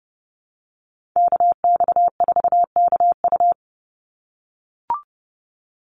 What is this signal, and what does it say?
Morse code, heard as a keyed tone, about 30 wpm, 700 Hz. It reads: KX4KU
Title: Call Signs - 30wpm